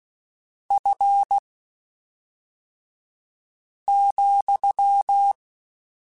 Morse code, heard as a keyed tone, about 16 wpm, spaced as for 3 wpm, 800 Hz. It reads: F,
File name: Spaced Characters / Spaced Characters-mp3s / Weather4 14 sp @5